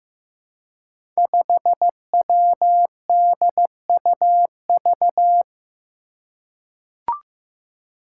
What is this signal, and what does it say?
Morse code, heard as a keyed tone, about 15 wpm, 700 Hz. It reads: 5WDUV
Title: Five Challenging Letter-Number Combinations 15wpm